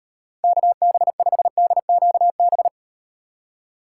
Morse code, heard as a keyed tone, about 38 wpm, 700 Hz. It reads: KB5DQB